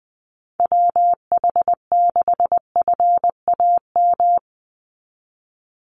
Morse code, heard as a keyed tone, about 20 wpm, 700 Hz. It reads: WH6FAM
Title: USA Call Signs 20wpm